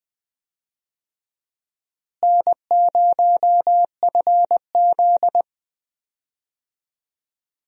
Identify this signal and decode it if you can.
Morse code, heard as a keyed tone, about 20 wpm, 700 Hz. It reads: N0FZ
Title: USA Call Signs - 4 Characters - Repeated 2x - 20wpm